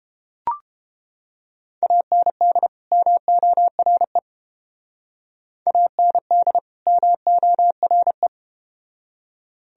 Morse code, heard as a keyed tone, about 31 wpm, 700 Hz. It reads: AND MORE AND MORE